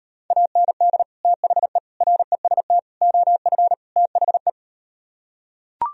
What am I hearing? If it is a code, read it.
Morse code, heard as a keyed tone, about 38 wpm, 700 Hz. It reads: AND THE REST OF THE